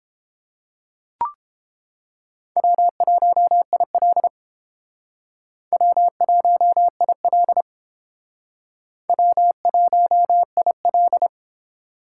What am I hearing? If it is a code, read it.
Morse code, heard as a keyed tone, about 30 wpm, 700 Hz. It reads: W1IL W1IL W1IL